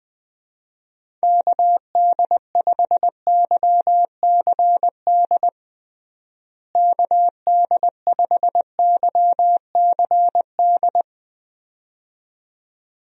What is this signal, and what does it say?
Morse code, heard as a keyed tone, about 20 wpm, 700 Hz. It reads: KD5YCD KD5YCD
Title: USA Call Signs - 6 Characters - Repeated 2x - 20wpm